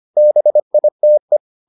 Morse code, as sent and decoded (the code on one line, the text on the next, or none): -... .. - .
BITE